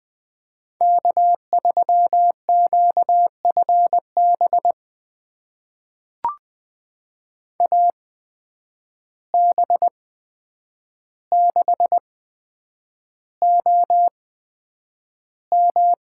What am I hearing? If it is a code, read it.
Morse code, heard as a keyed tone, about 20 wpm, spaced as for 9 wpm, 700 Hz. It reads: K3QFB AB6OM
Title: USA Call Signs - 5 Characters - 8x Character Spacing - 20wpm